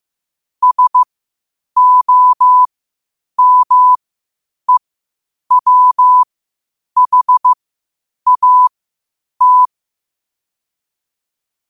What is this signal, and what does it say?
Morse code, heard as a keyed tone, about 15 wpm, spaced as for 8 wpm, 1000 Hz. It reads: SOMEWHAT
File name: qotdfiles / 8wpm